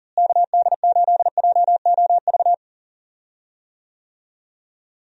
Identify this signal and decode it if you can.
Morse code, heard as a keyed tone, about 40 wpm, 700 Hz. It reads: KD8JOV
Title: USA Call Signs - 6 Characters - Repeated 3x - 40wpm